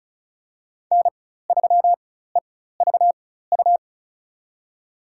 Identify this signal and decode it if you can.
Morse code, heard as a keyed tone, about 34 wpm, spaced as for 16 wpm, 700 Hz. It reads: N3EVU
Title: USA Call Signs - 4x Character Spacing - 35wpm